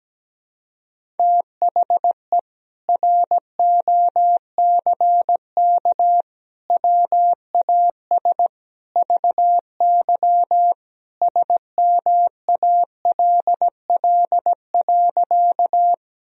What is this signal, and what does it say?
Morse code, heard as a keyed tone, about 17 wpm, 700 Hz. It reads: THE ROCK WAS VY SMALL.
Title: Sentences from Top 400 Words and Abbreviations 17wpm